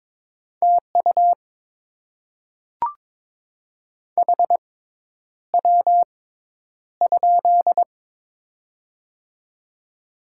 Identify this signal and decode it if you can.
Morse code, heard as a keyed tone, about 22 wpm, spaced as for 10 wpm, 700 Hz. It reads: TU HW?
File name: Top 20 QSO Elements - 6x Character Spacing - 22wpm